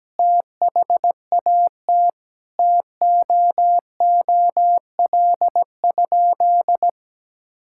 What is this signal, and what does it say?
Morse code, heard as a keyed tone, about 17 wpm, 700 Hz. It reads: THAT TOOL?
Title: Sentences from Top 1000 Words - Review Part 6 - 17wpm